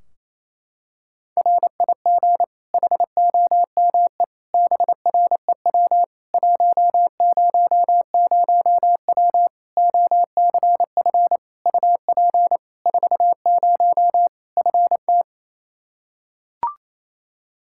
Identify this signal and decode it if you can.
Morse code, heard as a keyed tone, about 28 wpm, 700 Hz. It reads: RIG HOME BREW 100W OCF UP 40 FT
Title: K1USN SST QSO Practice 28wpm